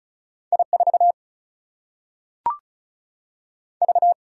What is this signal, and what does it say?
Morse code, heard as a keyed tone, about 36 wpm, 700 Hz. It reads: I4 V
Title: Single-Letter Single-Number 35wpm